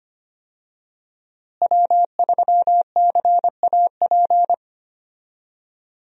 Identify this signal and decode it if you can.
Morse code, heard as a keyed tone, about 25 wpm, 700 Hz. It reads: W3CAP
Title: USA Call Signs - 5 Characters - Repeated 3x - 25wpm